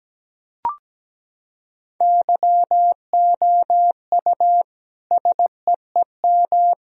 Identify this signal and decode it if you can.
Morse code, heard as a keyed tone, about 17 wpm, 700 Hz. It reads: YOU SEEM
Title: Sentences from Top 1000 Words - Review Part 3 - 17wpm